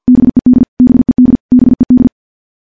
Morse code, heard as a keyed tone, about 50 wpm, 250 Hz. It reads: BED BED BED
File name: morse / morse-training-50WPM